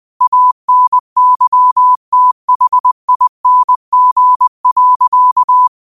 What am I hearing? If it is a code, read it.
Morse code, heard as a keyed tone, about 20 wpm, 1000 Hz. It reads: ANYTHING.
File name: qotdfiles / 20wpm